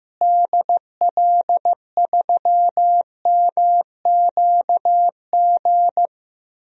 Morse code, heard as a keyed tone, about 16 wpm, 700 Hz. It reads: DL3MQG
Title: DXCC-100 Call Signs - 6 Characters - Repeated 3x - 15wpm